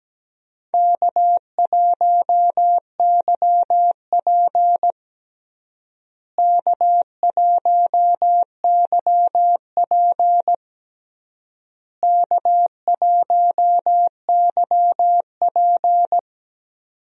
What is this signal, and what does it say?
Morse code, heard as a keyed tone, about 17 wpm, 700 Hz. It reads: K1YP K1YP K1YP
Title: USA Call Signs - 4 Characters - Repeated 3x - 17wpm